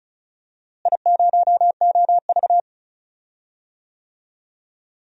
Morse code, heard as a keyed tone, about 36 wpm, 700 Hz. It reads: I0OV
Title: DXCC-100 Call Signs - 4 Characters - Repeated 2x - 35wpm